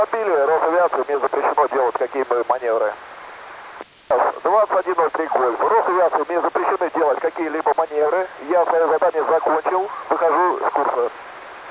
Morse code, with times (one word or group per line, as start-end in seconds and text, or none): none